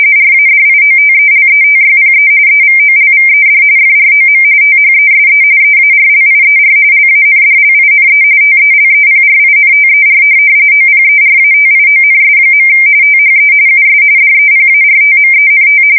RTTY, about 45.45 baud. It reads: RADIOTELETYPE EVOLVED FROM THESE EARLIER LANDLINE TELEPRINTER OPERATIONS. THE US DEPARTMENT